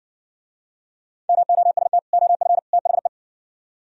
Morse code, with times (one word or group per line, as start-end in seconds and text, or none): 1.3-2.0 MOST
2.1-2.6 OF
2.7-3.1 THE